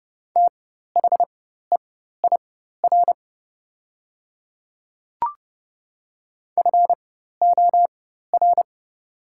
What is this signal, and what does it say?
Morse code, heard as a keyed tone, about 30 wpm, spaced as for 14 wpm, 700 Hz. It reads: THEIR FOR
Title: 4x Character Spacing - Top 200 Words - 30wpm